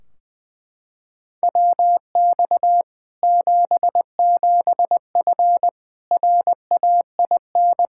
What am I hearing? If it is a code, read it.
Morse code, heard as a keyed tone, about 20 wpm, 700 Hz. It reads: WX 77F RAIN
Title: Practice QSO - Exchange 2 - Info - 20wpm